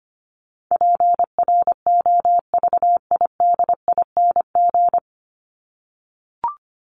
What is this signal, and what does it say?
Morse code, heard as a keyed tone, about 25 wpm, 700 Hz. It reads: PROVIDING
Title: ING Suffix - Top 100 Words with Base Words 25wpm